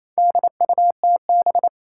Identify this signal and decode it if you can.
Morse code, heard as a keyed tone, about 28 wpm, 700 Hz. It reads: DUTB
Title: Four Challenging Letter-Number Combinations 28wpm